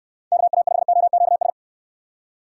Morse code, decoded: KA5OQS

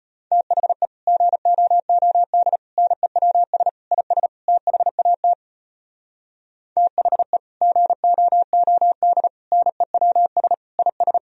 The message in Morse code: THE GOOD NEWS IS THAT THE GOOD NEWS IS

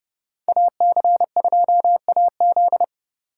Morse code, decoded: AC2AZ

